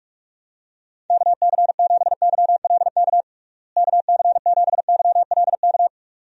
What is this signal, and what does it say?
Morse code, heard as a keyed tone, about 41 wpm, 700 Hz. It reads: KC7YLK KC7YLK
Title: USA Call Signs - 6 Characters - Repeated 2x - 45wpm